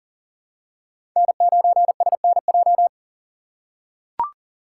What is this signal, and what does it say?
Morse code, heard as a keyed tone, about 40 wpm, 700 Hz. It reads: N9SNJ
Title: USA Call Signs - 5 Characters - 3x Character Spacing - 40wpm